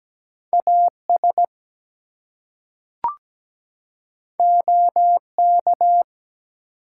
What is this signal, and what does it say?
Morse code, heard as a keyed tone, about 17 wpm, 700 Hz. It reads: AS OK
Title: Two Letter Words - 17wpm